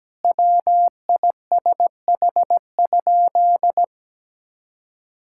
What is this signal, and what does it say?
Morse code, heard as a keyed tone, about 17 wpm, 700 Hz. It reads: WISH?
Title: Sentences from Top 1600 Words 17wpm